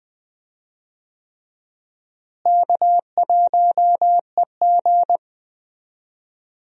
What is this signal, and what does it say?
Morse code, heard as a keyed tone, about 20 wpm, 700 Hz. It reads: K1EG
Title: USA Call Signs - 4 Characters - Repeated 3x - 20wpm